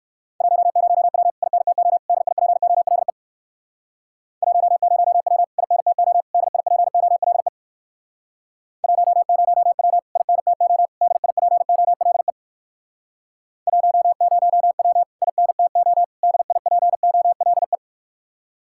Morse code, heard as a keyed tone, about 56 wpm, 700 Hz. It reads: 10W INTO DIPOLE 10W INTO DIPOLE 10W INTO DIPOLE 10W INTO DIPOLE
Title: Speed-Racing Common QSO Phrases Encore - 45wpm